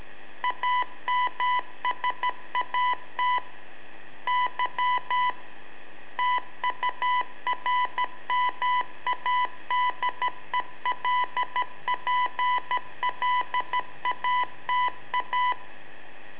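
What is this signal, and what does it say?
Morse code, heard as a keyed tone, about 13 wpm, 990 Hz. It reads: AMSAT Y TURMADELPLATA
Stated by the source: Telemetry in Morse from LU7AA and LU4EZ, AMSAT and Turma Mar del Plata, sent from a balloon.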